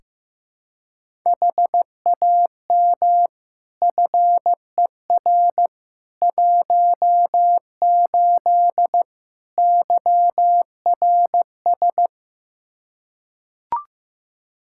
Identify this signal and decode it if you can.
Morse code, heard as a keyed tone, about 15 wpm, 700 Hz. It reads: HAM FER 18 YRS
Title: Practice QSO - Exchange 3 - More Info - 15wpm